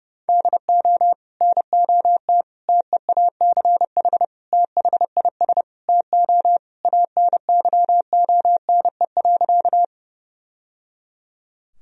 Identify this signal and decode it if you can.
Morse code, heard as a keyed tone, about 30 wpm, 700 Hz. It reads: DO NOT TEACH THIS TO ANYONE.
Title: Sentences from Top 1000 Words - Review Part 2 - 30wpm